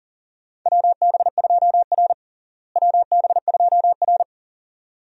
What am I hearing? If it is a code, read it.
Morse code, heard as a keyed tone, about 40 wpm, 700 Hz. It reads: WB2R WB2R
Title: USA Call Signs - 4 Characters - Repeated 2x - 40wpm